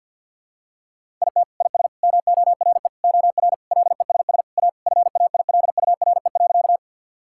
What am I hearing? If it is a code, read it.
Morse code, heard as a keyed tone, about 47 wpm, 700 Hz. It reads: IT IS MORE OR LESS A FAILURE.
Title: Sentences from Top 1000 Words - Review Part 6 - 50wpm